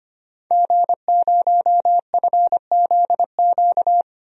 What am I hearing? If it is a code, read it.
Morse code, heard as a keyed tone, about 25 wpm, 700 Hz. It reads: G0FZQ
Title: DXCC-100 Call Signs - 5 Characters - 25wpm